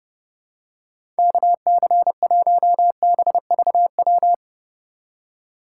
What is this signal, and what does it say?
Morse code, heard as a keyed tone, about 30 wpm, 700 Hz. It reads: KC1BVW
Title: USA Call Signs - 6 Characters - 3x Character Spacing - 30wpm